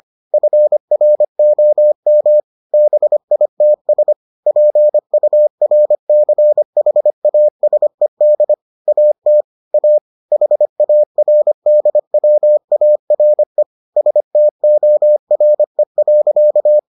Morse code, heard as a keyed tone, about 25 wpm, 600 Hz. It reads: FROM BITS PURCHASED AT A HARDWARE STORE.